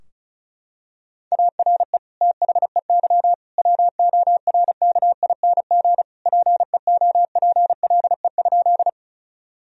Morse code, heard as a keyed tone, about 36 wpm, 700 Hz. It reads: ARE THEY WORKING PEOPLE?
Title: Sentences from Top 1000 Words - Review Part 4 - 35wpm